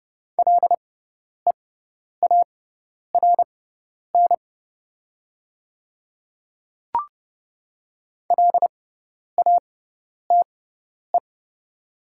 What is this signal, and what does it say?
Morse code, heard as a keyed tone, about 30 wpm, spaced as for 10 wpm, 700 Hz. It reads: LEARN LATE
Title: 6x Character Spacing - Top 300 Words - 30wpm